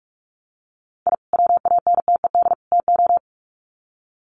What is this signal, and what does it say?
Morse code, heard as a keyed tone, about 44 wpm, 700 Hz. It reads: I WANTED TO
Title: Sets of 3 Words 45wpm